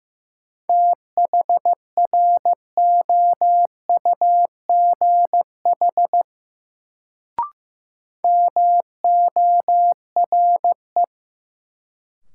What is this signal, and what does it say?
Morse code, heard as a keyed tone, about 15 wpm, 700 Hz. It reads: THROUGH MORE
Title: Sets of 2 Words - New Word Drill - 15wpm